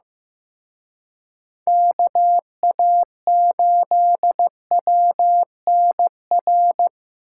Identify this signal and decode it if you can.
Morse code, heard as a keyed tone, about 18 wpm, 700 Hz. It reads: KA8WNR
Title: Speed Racing - USA 2x3 Call Signs 15wpm